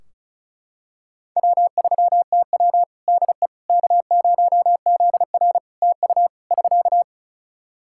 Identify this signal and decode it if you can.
Morse code, heard as a keyed tone, about 34 wpm, 700 Hz. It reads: W3TW DE K0ZR TU <SK>
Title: K1USN SST QSO Practice 35wpm